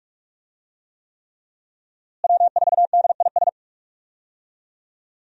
Morse code, heard as a keyed tone, about 43 wpm, 700 Hz. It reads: W4DIS
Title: USA Call Signs - 5 Characters - Repeated 3x - 45wpm